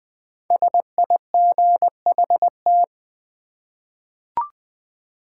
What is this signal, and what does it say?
Morse code, heard as a keyed tone, about 20 wpm, 700 Hz. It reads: SIGHT
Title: 5 Letter Words - First 3 Letters Given - 20wpm